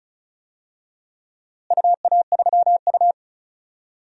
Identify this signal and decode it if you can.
Morse code, heard as a keyed tone, about 34 wpm, 700 Hz. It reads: UA3U
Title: DXCC-100 Call Signs - 4 Characters - Repeated 2x - 35wpm